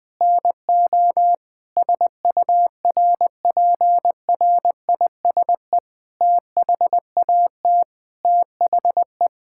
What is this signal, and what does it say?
Morse code, heard as a keyed tone, about 19 wpm, 700 Hz. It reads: NO SURPRISE THAT THE
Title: Sentences with 9-Word Phrases - 20wpm